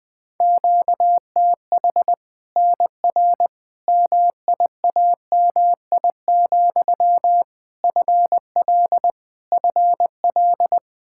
Morse code, heard as a keyed tone, about 20 wpm, 700 Hz. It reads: QTH NR MIAMI, FL FL
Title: Practice QSO - Exchange 1 - Beginning - 20wpm